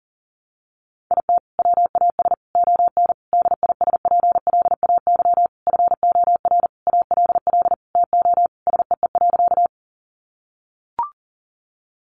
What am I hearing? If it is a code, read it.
Morse code, heard as a keyed tone, about 37 wpm, 700 Hz. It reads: IT WAS ON DISPLAY FOR ALL TO SEE.
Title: Sentences from Top 1200 Words 40wpm